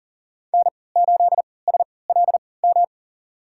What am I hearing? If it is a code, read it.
Morse code, heard as a keyed tone, about 40 wpm, spaced as for 23 wpm, 700 Hz. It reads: N8SLM
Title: USA Call Signs - 5 Characters - 3x Character Spacing - 40wpm